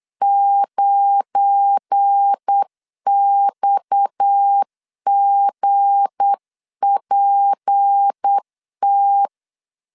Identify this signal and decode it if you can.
Morse code, heard as a keyed tone, about 8 wpm, 800 Hz. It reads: 9XGPT
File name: CW / DET / 1 9 1500C S B 8wpm 5624